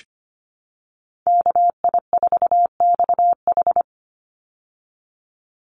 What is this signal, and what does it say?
Morse code, heard as a keyed tone, about 25 wpm, 700 Hz. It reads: KI4XH
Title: Call Signs Repeated 3x - 25wpm